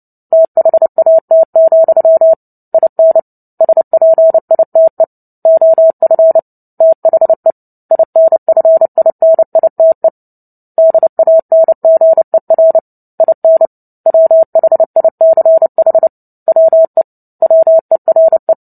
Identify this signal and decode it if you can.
Morse code, heard as a keyed tone, about 30 wpm, 650 Hz. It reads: THAT, IN SPITE OF THE INFINITE DANGER IN WHICH WE WERE
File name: Book Two / 03-Chapter 3 - 29 WPM